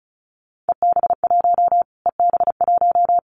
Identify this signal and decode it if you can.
Morse code, heard as a keyed tone, about 34 wpm, 700 Hz. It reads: EB1 EB1
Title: Callsign Prefixes for Top 25 DXCC - Repeated 3x - 35wpm